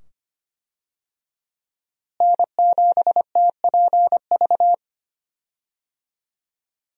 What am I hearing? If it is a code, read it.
Morse code, heard as a keyed tone, about 25 wpm, 700 Hz. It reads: N7TPV
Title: USA Call Signs - 5 Characters - Repeated 2x - 25wpm